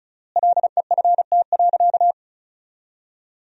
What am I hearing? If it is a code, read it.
Morse code, heard as a keyed tone, about 36 wpm, 700 Hz. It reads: LEFT.
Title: 3x Word-Spacing - Sentences from Top 300 Words - 35wpm